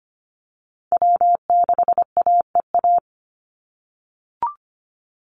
Morse code, heard as a keyed tone, about 25 wpm, 700 Hz. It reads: W6AEA
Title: Call Signs - 25wpm